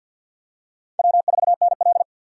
Morse code, heard as a keyed tone, about 53 wpm, 700 Hz. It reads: W4NL
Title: USA Call Signs - 6x Character Spacing - 50wpm